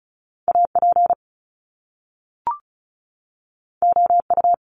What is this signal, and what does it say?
Morse code, heard as a keyed tone, about 36 wpm, 700 Hz. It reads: AP OU
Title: Top Two Letter Combinations in Words 35wpm